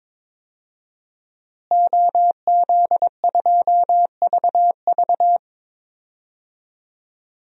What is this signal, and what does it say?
Morse code, heard as a keyed tone, about 22 wpm, 700 Hz. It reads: OZ2VV
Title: DXCC-100 Call Signs - 5 Characters - Repeated 2x - 22wpm